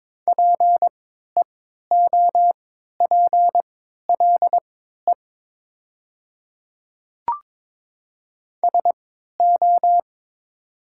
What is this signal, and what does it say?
Morse code, heard as a keyed tone, about 22 wpm, spaced as for 12 wpm, 700 Hz. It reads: PEOPLE SO
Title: 3x Character Spacing - Top 200 Words - 22wpm